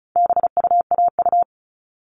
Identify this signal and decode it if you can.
Morse code, heard as a keyed tone, about 34 wpm, 700 Hz. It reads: BUAU
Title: All Cut Numbers - 4-Digit 35wpm